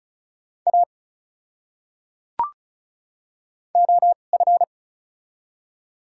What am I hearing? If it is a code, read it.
Morse code, heard as a keyed tone, about 36 wpm, spaced as for 25 wpm, 700 Hz. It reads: A OF
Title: 2x Character Spacing - Top 200 Words - 35wpm